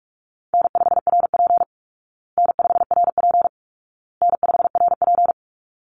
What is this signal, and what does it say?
Morse code, heard as a keyed tone, about 44 wpm, 700 Hz. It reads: N5RP N5RP N5RP
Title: Easy Call Signs Repeated 3x - 45wpm